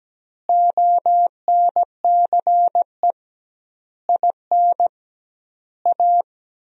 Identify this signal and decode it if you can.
Morse code, heard as a keyed tone, about 17 wpm, 700 Hz. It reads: ONCE IN A
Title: 2x Word-Spacing - Sentences from Top 500 Words - 17wpm